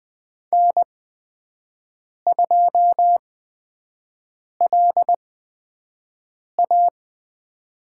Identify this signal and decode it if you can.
Morse code, heard as a keyed tone, about 20 wpm, spaced as for 5 wpm, 700 Hz. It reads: N2LA
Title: USA Call Signs - 4 Characters - 8x Character Spacing - 20wpm